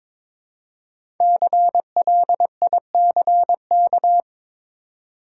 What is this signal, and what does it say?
Morse code, heard as a keyed tone, about 22 wpm, 700 Hz. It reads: CLICK
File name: 5 Letter Words - First 3 Letters Given - 22wpm